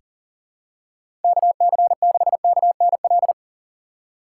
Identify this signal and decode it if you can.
Morse code, heard as a keyed tone, about 39 wpm, 700 Hz. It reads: KC6KNL
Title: USA Call Signs - 6 Characters - 3x Character Spacing - 40wpm